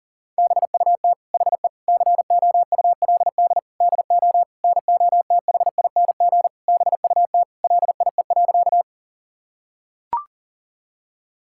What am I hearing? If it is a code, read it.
Morse code, heard as a keyed tone, about 40 wpm, 700 Hz. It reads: BUT HE COULD DO NOTHING BUT LIE.
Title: Sentences from Top 1600 Words 40wpm